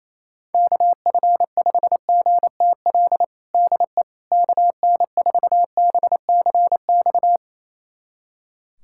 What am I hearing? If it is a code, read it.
Morse code, heard as a keyed tone, about 28 wpm, 700 Hz. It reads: KF5GTL DE KN4BCX
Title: USA Callsign DE Callsign 28wpm